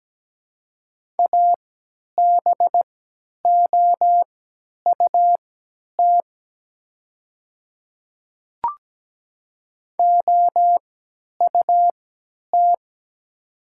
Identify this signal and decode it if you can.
Morse code, heard as a keyed tone, about 17 wpm, spaced as for 10 wpm, 700 Hz. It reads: ABOUT OUT
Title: 3x Character Spacing - Top 100 Words - 17wpm